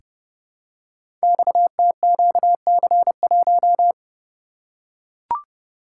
Morse code, heard as a keyed tone, about 31 wpm, 700 Hz. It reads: XTQC1
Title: Code Groups - 5 Letter:Numbers - 30wpm